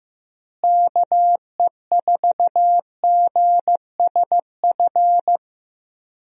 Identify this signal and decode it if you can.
Morse code, heard as a keyed tone, about 16 wpm, 700 Hz. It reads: KE4GSF
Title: USA Call Signs - 6 Characters - Repeated 2x - 15wpm